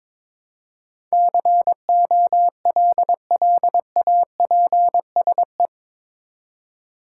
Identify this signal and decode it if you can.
Morse code, heard as a keyed tone, about 22 wpm, 700 Hz. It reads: COLLAPSE
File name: practice / 22wpm / Eight Letter Words - 22wpm